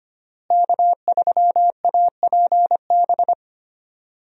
Morse code, heard as a keyed tone, about 25 wpm, 700 Hz. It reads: K3APB
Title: USA Call Signs - 5 Characters - Repeated 2x - 25wpm